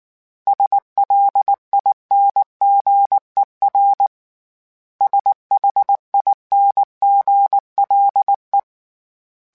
Morse code, heard as a keyed tone, about 19 wpm, 800 Hz. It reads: SLINGER SHINGLE
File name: Special INGs / Special INGs-mp3s / Hidden ings @5